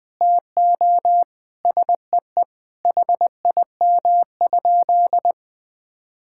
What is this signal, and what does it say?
Morse code, heard as a keyed tone, about 20 wpm, 700 Hz. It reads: TO SEE HIM?
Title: Sentences from Top 1600 Words 20wpm